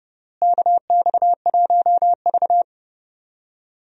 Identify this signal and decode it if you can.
Morse code, heard as a keyed tone, about 30 wpm, 700 Hz. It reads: KX1V
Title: USA Call Signs - 4 Characters - Repeated 3x - 30wpm